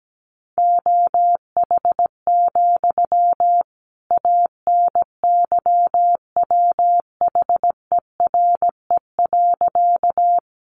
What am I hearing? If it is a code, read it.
Morse code, heard as a keyed tone, about 17 wpm, 700 Hz. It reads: OH, ANYWHERE.
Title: Sentences from Top 1400 Words 17wpm